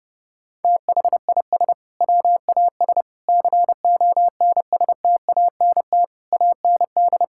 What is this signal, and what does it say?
Morse code, heard as a keyed tone, about 32 wpm, 700 Hz. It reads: THIS WAS CONSTANT AND
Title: Sentences from Top 1600 Words 30wpm